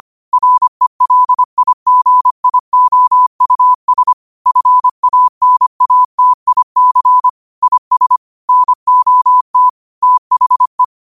Morse code, heard as a keyed tone, about 25 wpm, 1000 Hz. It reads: RELIGIOUS FANATIC IS NOT THE